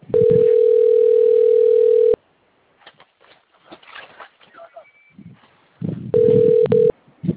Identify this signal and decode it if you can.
Ringback tone (North America), 440 + 480 Hz, with speech.
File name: out- - - - - .46789